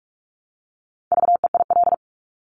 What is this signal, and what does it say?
Morse code, heard as a keyed tone, about 46 wpm, 700 Hz. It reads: VEIL
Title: Multi-letter Phonemes Repeated 3x 45wpm